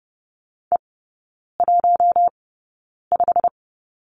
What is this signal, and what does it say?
Morse code, heard as a keyed tone, about 30 wpm, spaced as for 9 wpm, 700 Hz. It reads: E15
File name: course / 30:10wpm / 087 Lesson 15e - Number 5 - Three Characters 30wpm